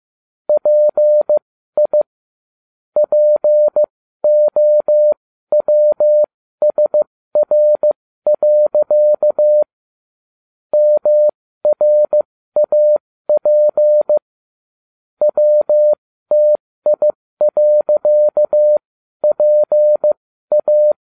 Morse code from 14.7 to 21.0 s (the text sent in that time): WTI.PA